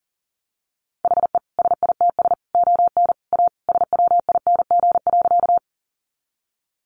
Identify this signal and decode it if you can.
Morse code, heard as a keyed tone, about 40 wpm, 700 Hz. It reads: HE SITS ON A SWING.
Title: Additional Easy Phrases 40wpm